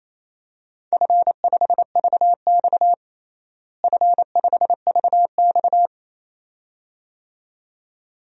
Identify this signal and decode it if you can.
Morse code, heard as a keyed tone, about 28 wpm, 700 Hz. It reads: F5VX F5VX
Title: DXCC-100 Call Signs - 4 Characters - Repeated 2x - 28wpm